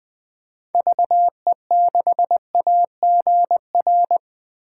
Morse code, heard as a keyed tone, about 20 wpm, 700 Hz. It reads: VE6AGR
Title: 2023 SOTA Chaser Call Signs - 20wpm